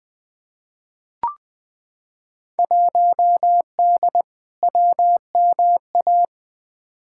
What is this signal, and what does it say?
Morse code, heard as a keyed tone, about 20 wpm, 700 Hz. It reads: 1D WMA
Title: Field Day Exchanges - 20wpm